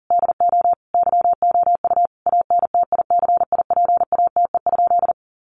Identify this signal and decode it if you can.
Morse code, heard as a keyed tone, about 40 wpm, 700 Hz. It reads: DO YOU ANTICIPATE?
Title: Sentences from Top 1500 Words 40wpm